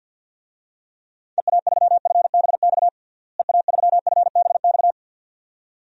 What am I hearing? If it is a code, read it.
Morse code, heard as a keyed tone, about 52 wpm, 700 Hz. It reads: EA3FBX EA3FBX
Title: DXCC-100 Call Signs - 6 Characters - Repeated 2x - 50wpm